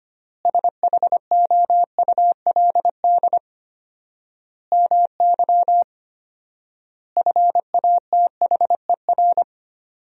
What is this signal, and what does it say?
Morse code, heard as a keyed tone, about 25 wpm, 700 Hz. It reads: SHOULD MY FATHER
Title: 4x Word-Spacing - Sentences from Top 500 Words - 25wpm